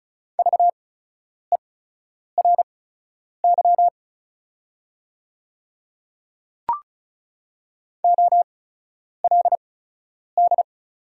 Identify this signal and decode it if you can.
Morse code, heard as a keyed tone, about 36 wpm, spaced as for 10 wpm, 690 Hz. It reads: VERY OLD
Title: 8x Character Spacing - Top 200 Words - 35wpm